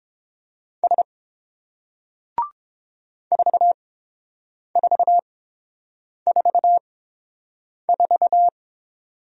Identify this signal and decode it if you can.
Morse code, heard as a keyed tone, about 30 wpm, spaced as for 7 wpm, 700 Hz. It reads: S 4444